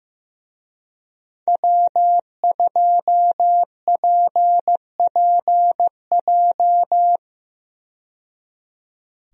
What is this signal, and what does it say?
Morse code, heard as a keyed tone, about 15 wpm, 700 Hz. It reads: W2PPJ
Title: USA Call Signs - 5 Characters - 15wpm